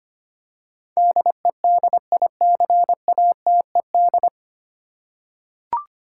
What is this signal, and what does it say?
Morse code, heard as a keyed tone, about 25 wpm, 700 Hz. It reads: DEDICATED
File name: Nine Letter Words - 25wpm